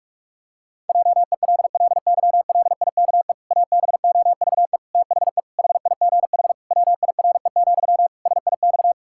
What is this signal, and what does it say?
Morse code, heard as a keyed tone, about 46 wpm, 700 Hz. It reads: JELLYLIKE ABOVE THE HIGH WIRE, SIX
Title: Pangram Challenge v2 - 45wpm